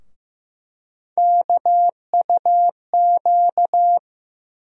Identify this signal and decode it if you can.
Morse code, heard as a keyed tone, about 15 wpm, 700 Hz. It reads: KUQ
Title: USA Call Sign Postfixes - 3 Letters 15wpm